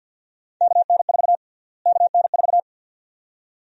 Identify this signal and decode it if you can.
Morse code, heard as a keyed tone, about 52 wpm, 700 Hz. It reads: KN4 KN4
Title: USA Call Sign Prefixes - Repeated 2x - 50wpm